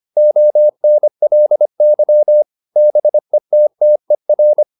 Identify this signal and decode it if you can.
Morse code, heard as a keyed tone, about 25 wpm, 600 Hz. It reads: ONLY BETTER